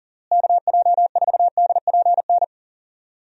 Morse code, heard as a keyed tone, about 40 wpm, 700 Hz. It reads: KJ4DPN
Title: USA Call Signs - 6 Characters - Repeated 3x - 40wpm